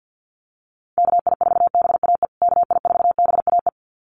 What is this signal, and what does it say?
Morse code, heard as a keyed tone, about 48 wpm, 700 Hz. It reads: KI4BAE KI4BAE
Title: USA Call Signs Repeated 2x 50wpm